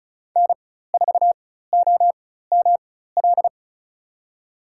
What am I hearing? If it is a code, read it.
Morse code, heard as a keyed tone, about 34 wpm, spaced as for 16 wpm, 700 Hz. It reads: N4OML
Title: USA Call Signs - 5 Characters - 4x Character Spacing - 35wpm